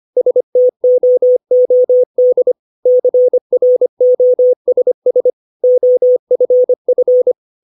Morse code, heard as a keyed tone, about 25 wpm, 500 Hz. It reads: STOOD CROSS OFF
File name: 632 words / 632 Words 25WPM